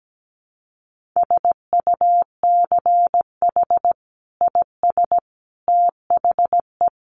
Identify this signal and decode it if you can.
Morse code, heard as a keyed tone, about 17 wpm, 700 Hz. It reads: SUCH IS THE
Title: Sentences from Top 1400 Words 17wpm